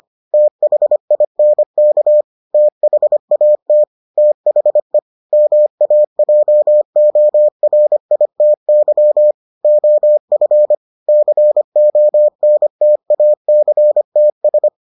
Morse code, heard as a keyed tone, about 25 wpm, 600 Hz. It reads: THINK THAT THE MAJORITY OF CONTACTS